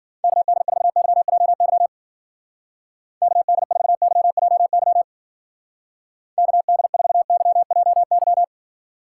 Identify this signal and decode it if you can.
Morse code, heard as a keyed tone, about 52 wpm, 700 Hz. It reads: KD4YJY KD4YJY KD4YJY